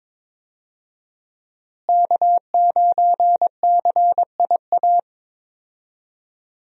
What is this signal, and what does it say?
Morse code, heard as a keyed tone, about 22 wpm, 700 Hz. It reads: K9CIA
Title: USA Call Signs - 5 Characters - Repeated 3x - 22wpm